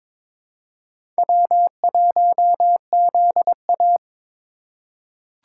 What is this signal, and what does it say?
Morse code, heard as a keyed tone, about 22 wpm, 700 Hz. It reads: W1ZA
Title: USA Call Signs - 3x Character Spacing - 22wpm